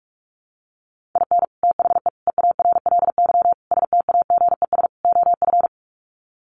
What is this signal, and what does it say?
Morse code, heard as a keyed tone, about 44 wpm, 700 Hz. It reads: IN THE EARLY STAGES OF
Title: Sets of 5 Words Encore 45wpm